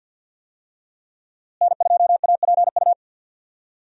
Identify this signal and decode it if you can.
Morse code, heard as a keyed tone, about 51 wpm, 700 Hz. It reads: N2APU
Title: USA Call Signs - 5 Characters - Repeated 2x - 50wpm